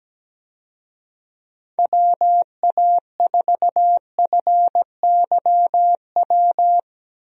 Morse code, heard as a keyed tone, about 17 wpm, 700 Hz. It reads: WA4FYW
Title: USA Call Signs - 6 Characters - 17wpm